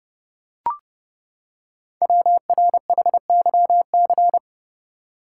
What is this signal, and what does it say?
Morse code, heard as a keyed tone, about 30 wpm, 690 Hz. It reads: WRHYC
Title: Code Groups - 5 Letters - 30wpm